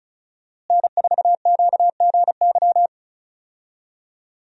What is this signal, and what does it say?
Morse code, heard as a keyed tone, about 34 wpm, 700 Hz. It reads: N4QGY
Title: USA Call Signs - 5 Characters - Repeated 2x - 35wpm